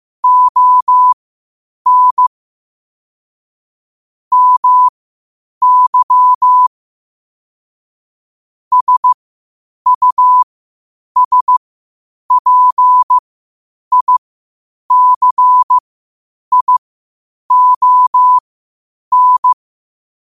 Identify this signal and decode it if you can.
Morse code, heard as a keyed tone, about 15 wpm, spaced as for 8 wpm, 1000 Hz. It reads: ON MY SUSPICION
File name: qotdfiles / 8wpm